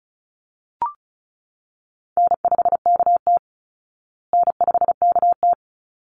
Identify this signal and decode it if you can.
Morse code, heard as a keyed tone, about 34 wpm, 700 Hz. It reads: N5KT N5KT
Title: Call Signs Repeated 3x - 35wpm